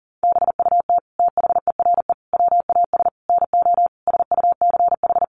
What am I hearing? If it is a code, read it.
Morse code, heard as a keyed tone, about 40 wpm, 700 Hz. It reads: BUT THERE WAS NO SUCH
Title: Sentences from Top 1500 Words 40wpm